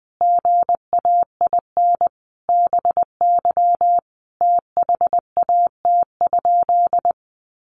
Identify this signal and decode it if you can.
Morse code, heard as a keyed tone, about 20 wpm, 700 Hz. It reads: GAIN BY THAT?
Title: Sentences from Top 1100 Words 20wpm